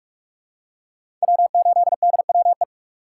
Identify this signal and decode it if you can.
Morse code, heard as a keyed tone, about 46 wpm, 690 Hz. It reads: W8DWE
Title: USA Call Signs - 5 Characters - 6x Character Spacing - 45wpm